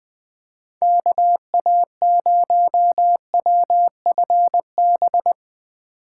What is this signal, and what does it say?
Morse code, heard as a keyed tone, about 20 wpm, 700 Hz. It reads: KA0WFB
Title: Speed Racing - USA 2x3 Call Signs 17wpm